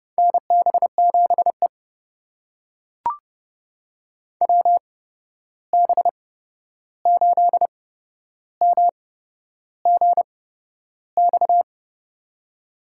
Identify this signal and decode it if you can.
Morse code, heard as a keyed tone, about 32 wpm, spaced as for 11 wpm, 700 Hz. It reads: NB7E WB8MGX
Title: USA Call Signs - 8x Character Spacing - 30wpm